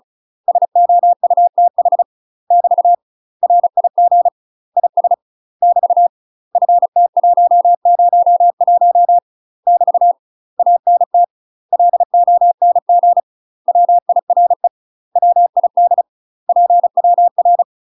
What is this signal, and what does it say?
Morse code, heard as a keyed tone, about 36 wpm, 700 Hz. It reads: SOUTH <BT> RIG IS <BT> FT101 <BT> ANT LONG WIRE WID PWR
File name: UK MASTER QSOs TRAINING / 35WPM / UK MASTER QSOs 0010